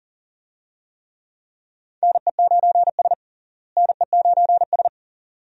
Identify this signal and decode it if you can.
Morse code, heard as a keyed tone, about 40 wpm, 700 Hz. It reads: NE9S NE9S
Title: USA Call Signs - 4 Characters - Repeated 2x - 40wpm